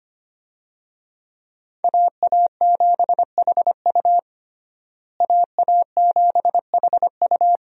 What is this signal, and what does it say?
Morse code, heard as a keyed tone, about 25 wpm, 700 Hz. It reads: AA7HU AA7HU
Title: USA Call Signs - 5 Characters - Repeated 2x - 25wpm